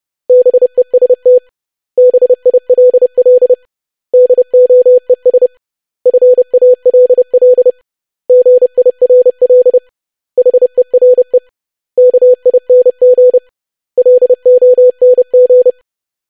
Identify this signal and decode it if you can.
Morse code, heard as a keyed tone, about 30 wpm, 500 Hz. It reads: BEST BILL DOES FALL GIRL HERE KING LONG
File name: basic500-25wpm